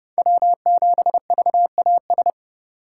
Morse code, heard as a keyed tone, about 30 wpm, 700 Hz. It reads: W7VAS
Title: K1USN SST QSO Practice - 30wpm